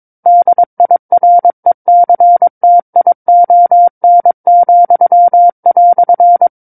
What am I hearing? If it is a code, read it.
Morse code, heard as a keyed tone, about 22 wpm, 700 Hz. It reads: DIRECTION,"
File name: 06-Chapter-06 22 WPM